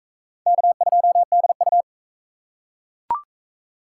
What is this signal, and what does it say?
Morse code, heard as a keyed tone, about 41 wpm, 700 Hz. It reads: K2DU